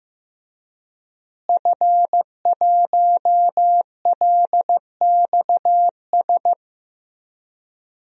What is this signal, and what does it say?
Morse code, heard as a keyed tone, about 15 wpm, 700 Hz. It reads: F1LXS
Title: DXCC-100 Call Signs - 5 Characters - Repeated 3x - 15wpm